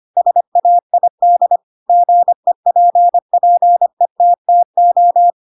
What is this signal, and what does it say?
Morse code, heard as a keyed tone, about 25 wpm, 700 Hz. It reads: SAID GEPPETTO
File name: Pinocchio / Pinocchio ch3 25 25 ik1hge